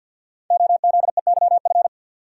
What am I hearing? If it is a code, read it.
Morse code, heard as a keyed tone, about 52 wpm, 700 Hz. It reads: OZEYF
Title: Code Groups - 5 Letter:Numbers - 50wpm